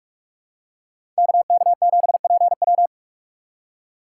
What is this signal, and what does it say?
Morse code, heard as a keyed tone, about 46 wpm, 700 Hz. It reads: KK7PW
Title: USA Call Signs - 5 Characters - 3x Character Spacing - 45wpm